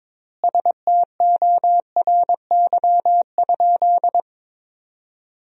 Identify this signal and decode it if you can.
Morse code, heard as a keyed tone, about 22 wpm, 700 Hz. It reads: STORY?
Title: 4x Word-Spacing - Sentences from Top 400 Words - 22wpm